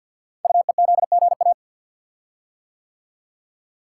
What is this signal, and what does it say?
Morse code, heard as a keyed tone, about 46 wpm, 690 Hz. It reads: UEZGA
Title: Code Groups - 5 Letter:Numbers - 50wpm